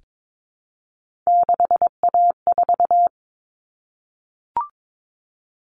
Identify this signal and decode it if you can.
Morse code, heard as a keyed tone, about 22 wpm, 700 Hz. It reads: 6A4
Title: Common Cut Numbers - 3-Digit 22wpm